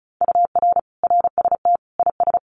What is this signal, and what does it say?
Morse code, heard as a keyed tone, about 34 wpm, 700 Hz. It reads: UR RST IS
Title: Common QSO Phrases Encore 35wpm